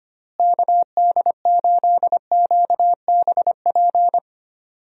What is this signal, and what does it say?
Morse code, heard as a keyed tone, about 25 wpm, 700 Hz. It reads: KD8QBP
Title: USA Call Signs - 6 Characters - Repeated 3x - 25wpm